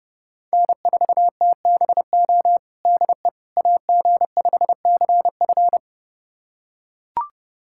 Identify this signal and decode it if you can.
Morse code, heard as a keyed tone, about 31 wpm, 700 Hz. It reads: N4TBO DE AG5CF
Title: USA Callsign DE Callsign - 5 Characters - 30wpm